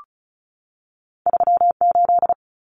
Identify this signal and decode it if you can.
Morse code, heard as a keyed tone, about 34 wpm, 700 Hz. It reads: 38
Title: Numbers - 2-Digit 35wpm